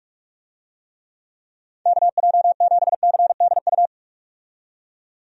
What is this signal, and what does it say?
Morse code, heard as a keyed tone, about 46 wpm, 700 Hz. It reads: KJ7CDU
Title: USA Call Signs - 6 Characters - 45wpm